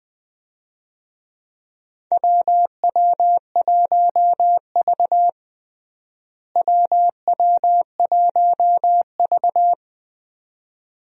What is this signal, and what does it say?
Morse code, heard as a keyed tone, about 20 wpm, 700 Hz. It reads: WW1V WW1V
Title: USA Call Signs - 4 Characters - Repeated 2x - 20wpm